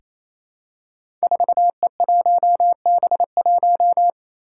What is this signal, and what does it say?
Morse code, heard as a keyed tone, about 28 wpm, 700 Hz. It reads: 4E1B1
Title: Five Challenging Letter-Number Combinations 28wpm